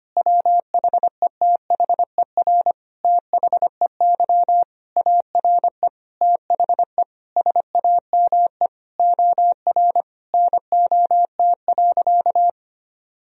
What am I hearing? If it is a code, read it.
Morse code, heard as a keyed tone, about 25 wpm, 700 Hz. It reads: WHETHER THEY ARE THE SAME OR NOT.
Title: Sentences from Top 1000 Words - Review Part 4 - 25wpm